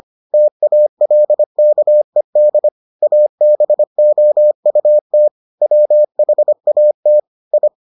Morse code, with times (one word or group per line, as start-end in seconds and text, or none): 0.3-2.7 TALKED
3.0-5.3 ABOUT
5.6-7.2 WHAT
7.5-7.7 I